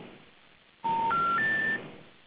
Special information tone.